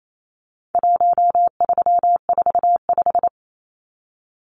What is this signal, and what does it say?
Morse code, heard as a keyed tone, about 28 wpm, 700 Hz. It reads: 1345
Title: Numbers - 4-Digit 28wpm